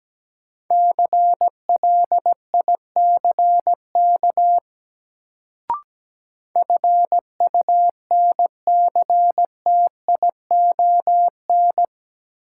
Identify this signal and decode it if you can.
Morse code, heard as a keyed tone, about 17 wpm, 700 Hz. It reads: CLICK FUNCTION
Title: Sentences from Top 1100 Words - New Word Drill - 17wpm